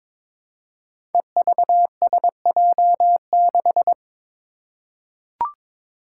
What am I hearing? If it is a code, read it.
Morse code, heard as a keyed tone, about 22 wpm, 700 Hz. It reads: EVSJ6
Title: Five Challenging Letter-Number Combinations 22wpm